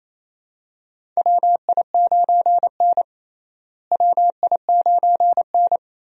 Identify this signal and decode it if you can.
Morse code, heard as a keyed tone, about 28 wpm, 700 Hz. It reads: WI9N WI9N
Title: USA Call Signs - 4 Characters - Repeated 3x - 28wpm